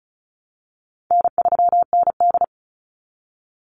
Morse code, heard as a keyed tone, about 34 wpm, 700 Hz. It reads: N3ND
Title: Easy Call Signs Repeated 2x - 35wpm